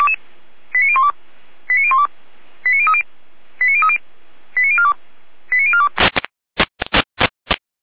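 Five-tone selective call (ZVEI), heard as quick runs of five tones, tones of about 70 ms. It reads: 89012 89012 89030 89030 89042 89042
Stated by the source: ZVEI five-tone selective call received with an SDR.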